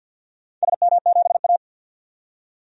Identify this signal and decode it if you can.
Morse code, heard as a keyed tone, about 53 wpm, 700 Hz. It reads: SM7A
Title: DXCC-100 Call Signs - 4 Characters - 50wpm